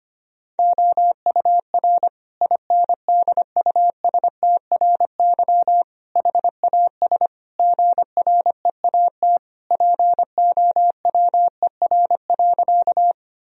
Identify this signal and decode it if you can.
Morse code, heard as a keyed tone, about 25 wpm, 700 Hz. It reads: OUR INDUSTRY HAS GREAT POWER.
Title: Sentences from Top 1000 Words - Review Part 4 - 25wpm